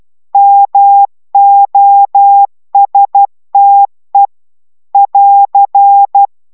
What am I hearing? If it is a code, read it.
Morse code, heard as a keyed tone, about 12 wpm, 800 Hz. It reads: MOSTE <AR>